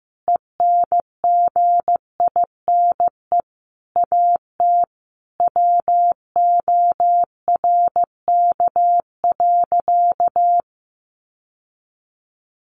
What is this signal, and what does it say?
Morse code, heard as a keyed tone, about 15 wpm, 700 Hz. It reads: ENGINE AT WORK.
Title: Sentences from Top 1500 Words 15wpm